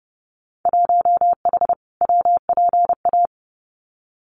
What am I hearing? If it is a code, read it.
Morse code, heard as a keyed tone, about 29 wpm, 700 Hz. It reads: 1H WPA
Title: Winter Field Day Exchanges 30wpm